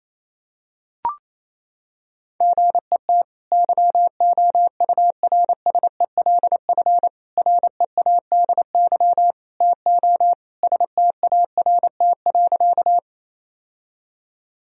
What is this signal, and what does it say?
Morse code, heard as a keyed tone, about 28 wpm, 700 Hz. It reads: GET YOURSELF READY TO START.
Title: Sentences from Top 1000 Words - Review Part 1 - 28wpm